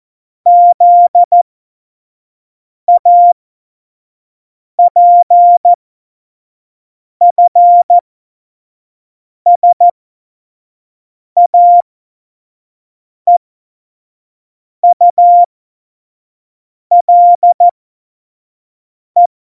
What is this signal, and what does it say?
Morse code, heard as a keyed tone, about 14 wpm, spaced as for 5 wpm, 700 Hz. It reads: ZAPFSAEULE
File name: Morsekurs / Witze / Witz 21. November 2017, 25 F70